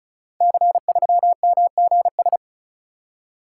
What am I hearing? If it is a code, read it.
Morse code, heard as a keyed tone, about 36 wpm, 700 Hz. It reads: C3MGS
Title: Code Groups - 5 Letter:Numbers - 35wpm